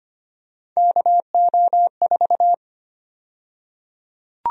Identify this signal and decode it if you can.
Morse code, heard as a keyed tone, about 25 wpm, 700 Hz. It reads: KO4
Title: USA Call Sign Prefixes - 25wpm